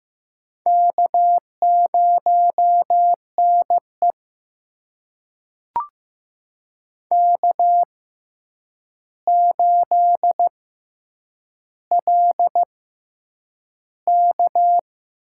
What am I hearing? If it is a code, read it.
Morse code, heard as a keyed tone, about 15 wpm, spaced as for 8 wpm, 700 Hz. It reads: K0NE K8LK
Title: USA Call Signs - 4 Characters - 6x Character Spacing - 15wpm